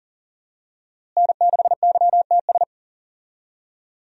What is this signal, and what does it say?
Morse code, heard as a keyed tone, about 40 wpm, 700 Hz. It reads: N6YTS
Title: USA Call Signs - 5 Characters - Repeated 3x - 40wpm